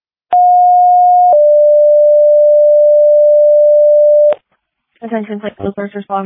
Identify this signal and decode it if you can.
Two-tone page, heard as a one-second tone, then a three-second tone, about 720 then 590 Hz, with speech. It is heard from 0.3 to 4.3 s.